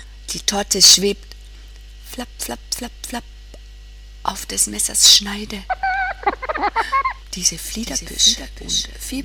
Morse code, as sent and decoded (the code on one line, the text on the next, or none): none